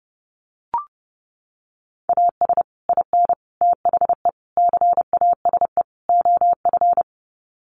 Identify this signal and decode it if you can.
Morse code, heard as a keyed tone, about 30 wpm, 700 Hz. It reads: AS IN THE CASE OF
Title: Sets of 5 Words 30wpm